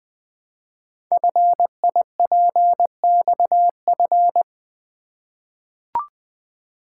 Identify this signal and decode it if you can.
Morse code, heard as a keyed tone, about 20 wpm, 700 Hz. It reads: FIPXF
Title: Code Groups - 5 Letters - 20wpm